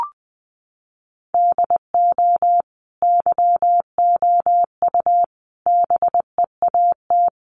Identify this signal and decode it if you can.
Morse code, heard as a keyed tone, about 20 wpm, 700 Hz. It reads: DO YOU BEAT
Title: Sentences from Top 1300 Words 20wpm